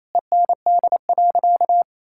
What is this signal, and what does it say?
Morse code, heard as a keyed tone, about 28 wpm, 700 Hz. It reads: END.
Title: Sentences from Top 1000 Words - Review Part 5 - 28wpm